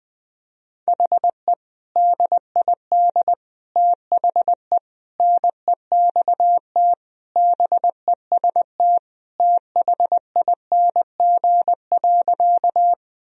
Morse code, heard as a keyed tone, about 20 wpm, 700 Hz. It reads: HE DID THE NEXT BEST THING.
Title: Sentences from Top 1000 Words - Review Part 1 - 20wpm